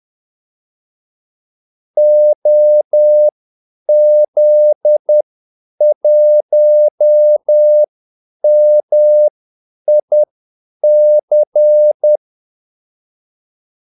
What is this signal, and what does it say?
Morse code, heard as a keyed tone, about 10 wpm, spaced as for 8 wpm, 600 Hz. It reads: OZ1MIC
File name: Callsigns-10-8wpm